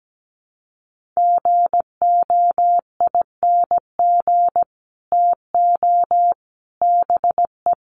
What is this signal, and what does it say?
Morse code, heard as a keyed tone, about 17 wpm, 700 Hz. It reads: GOING TO BE
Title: Sets of 5 Words Encore2 17wpm